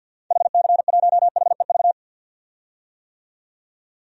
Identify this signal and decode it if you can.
Morse code, heard as a keyed tone, about 51 wpm, 700 Hz. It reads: HC1HEV
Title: DXCC-100 Call Signs - 6 Characters - 50wpm